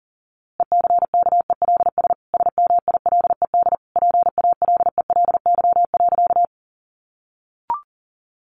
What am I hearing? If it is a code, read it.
Morse code, heard as a keyed tone, about 40 wpm, 700 Hz. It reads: ECKELS SMILED PALELY.
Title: A Sound Of Thunder 40wpm